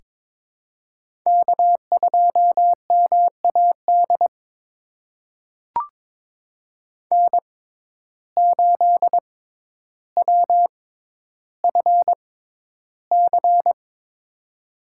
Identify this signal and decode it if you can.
Morse code, heard as a keyed tone, about 22 wpm, spaced as for 11 wpm, 700 Hz. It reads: K2MAD N8WFC
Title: USA Call Signs - 5 Characters - 6x Character Spacing - 22wpm